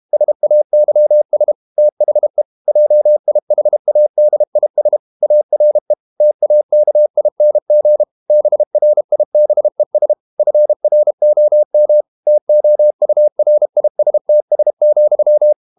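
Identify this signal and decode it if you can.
Morse code, heard as a keyed tone, about 32 wpm, 600 Hz. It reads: SAYS THE JIHADIS ARE TAKING BRIBES FROM TOURISTS,
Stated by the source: Morse code practice at 32 wpm from CW med Gnister, reading Ken Follett's novel Never in English.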